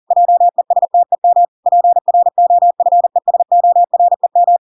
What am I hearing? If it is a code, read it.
Morse code, heard as a keyed tone, about 40 wpm, 700 Hz. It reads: JESTEM PROFESOREM